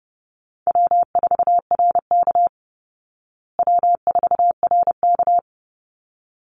Morse code, heard as a keyed tone, about 30 wpm, 700 Hz. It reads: W4RK W4RK
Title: Easy Call Signs Repeated 2x - 30wpm